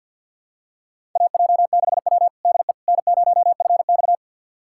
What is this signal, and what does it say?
Morse code, heard as a keyed tone, about 50 wpm, 700 Hz. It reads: AJ6W DE N0FX
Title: USA Callsign DE Callsign - 4 Characters - 50wpm